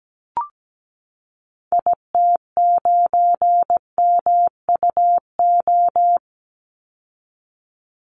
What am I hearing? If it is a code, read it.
Morse code, heard as a keyed tone, about 17 wpm, 700 Hz. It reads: IT9MUO
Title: Call Signs Repeated 2x - 17wpm